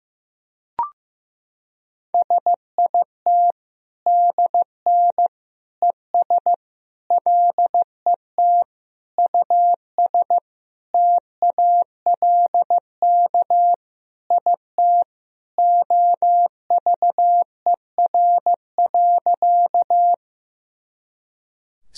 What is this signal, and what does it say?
Morse code, heard as a keyed tone, about 15 wpm, 700 Hz. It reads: SIT DN ES LET US TALK IT OVER.
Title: Sentences from Top 500 Words and Abbreviations 15wpm